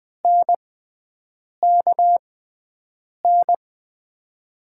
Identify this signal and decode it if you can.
Morse code, heard as a keyed tone, about 20 wpm, spaced as for 7 wpm, 700 Hz. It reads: NKN
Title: USA Call Signs - 6 Characters - 6x Character Spacing - 20wpm